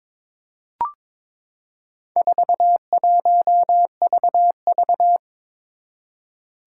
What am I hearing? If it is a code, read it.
Morse code, heard as a keyed tone, about 22 wpm, 700 Hz. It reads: 41VV
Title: Four Challenging Letter-Number Combinations 22wpm